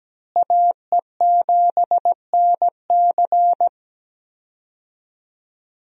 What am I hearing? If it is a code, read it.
Morse code, heard as a keyed tone, about 17 wpm, 700 Hz. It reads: AE7NC
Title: USA Call Signs - 5 Characters - Repeated 2x - 17wpm